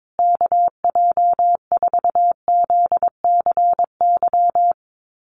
Morse code, heard as a keyed tone, about 22 wpm, 700 Hz. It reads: KJ4ZCY
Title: USA Call Signs Repeated 2x 22wpm